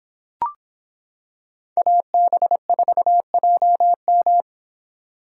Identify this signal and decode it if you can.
Morse code, heard as a keyed tone, about 26 wpm, 700 Hz. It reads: AB4JM